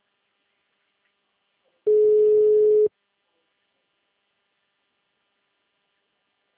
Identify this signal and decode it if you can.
Ringback tone (Europe, 425 Hz).
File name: out- -2044- - - .16